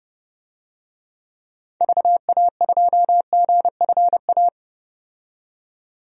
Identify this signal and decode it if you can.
Morse code, heard as a keyed tone, about 30 wpm, 700 Hz. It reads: VA2GFA
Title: DXCC-100 Call Signs - 6 Characters - 30wpm